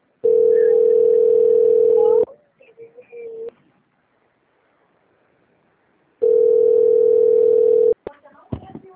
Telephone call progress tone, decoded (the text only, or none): ringback tone